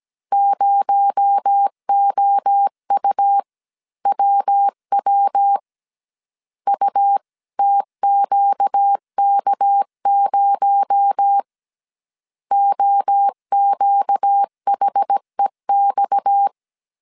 Morse code, heard as a keyed tone, about 17 wpm, 800 Hz. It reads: 0OUWW UTQK0 OQHEX
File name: CW / DET / 1 6 1500C S A 16wpm 2635